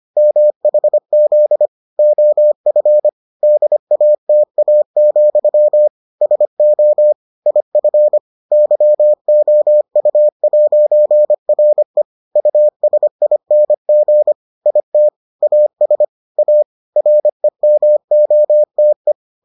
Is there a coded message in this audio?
Yes: Morse code from 0.2 to 19.1 s, heard as a keyed tone, about 25 wpm, 600 Hz. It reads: MHZ OF DATA, SO IF YOU'RE USING IT AS A REMOTE